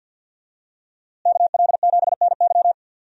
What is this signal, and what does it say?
Morse code, heard as a keyed tone, about 47 wpm, 700 Hz. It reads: KL7NY
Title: USA Call Signs - 5 Characters - Repeated 3x - 50wpm